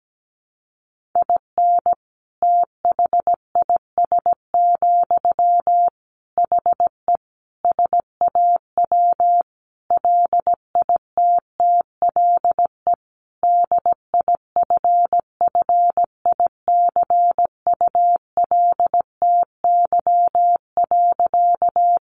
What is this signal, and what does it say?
Morse code, heard as a keyed tone, about 17 wpm, 700 Hz. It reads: IN THIS, HE SAW LITTLE DIFFICULTY.
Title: Sentences from Top 1500 Words 17wpm